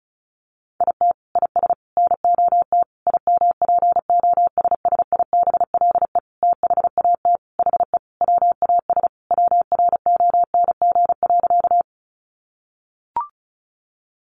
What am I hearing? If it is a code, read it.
Morse code, heard as a keyed tone, about 34 wpm, 700 Hz. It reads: IT IS NOT IMPOSSIBLE THAT HE WAS WRONG.
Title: Sentences from Top 1100 Words 35wpm